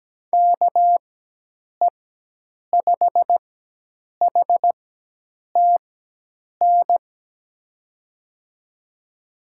Morse code, heard as a keyed tone, about 17 wpm, spaced as for 8 wpm, 700 Hz. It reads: KE5HTN
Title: USA Call Signs - 4x Character Spacing - 17wpm